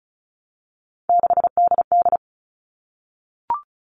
Morse code, heard as a keyed tone, about 34 wpm, 700 Hz. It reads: 6DD
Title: All Cut Numbers - 3-Digit 35wpm